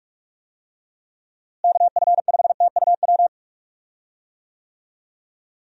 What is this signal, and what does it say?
Morse code, heard as a keyed tone, about 46 wpm, 700 Hz. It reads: KF5TUW
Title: USA Call Signs - 6 Characters - Repeated 3x - 45wpm